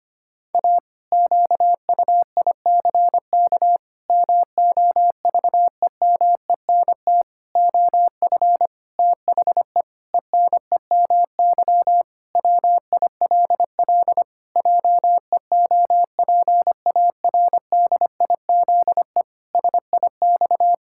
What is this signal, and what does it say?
Morse code, heard as a keyed tone, about 25 wpm, 700 Hz. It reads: A QUICK MOVEMENT OF THE ENEMY WILL JEOPARDIZE SIX
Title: Pangram Challenge v2 - 25wpm